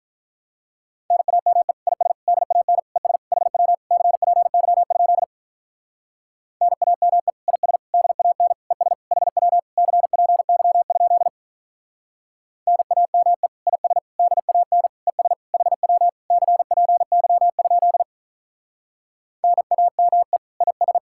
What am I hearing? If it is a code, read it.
Morse code, heard as a keyed tone, about 45 wpm, 700 Hz. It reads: NAME IS DAN ES HW CPY? NAME IS DAN ES HW CPY? NAME IS DAN ES HW CPY? NAME IS